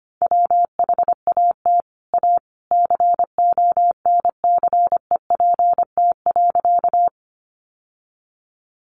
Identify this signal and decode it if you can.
Morse code, heard as a keyed tone, about 25 wpm, 700 Hz. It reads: WHAT A CONCEPT.
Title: Sentences from Top 1100 Words 25wpm